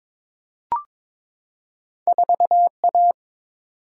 Morse code, heard as a keyed tone, about 22 wpm, 700 Hz. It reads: 4A
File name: practice / 22wpm / Two Letter-Number Combinations - 22wpm